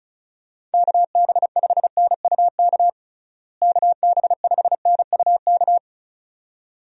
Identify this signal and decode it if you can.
Morse code, heard as a keyed tone, about 34 wpm, 700 Hz. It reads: KB5NUK KB5NUK
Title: USA Call Signs - 6 Characters - Repeated 2x - 35wpm